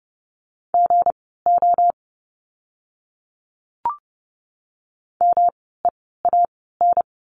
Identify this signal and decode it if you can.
Morse code, heard as a keyed tone, about 30 wpm, spaced as for 17 wpm, 700 Hz. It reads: GO MEAN
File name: course / 30:10wpm / 237 Lesson 38f - Letter q - Words 30wpm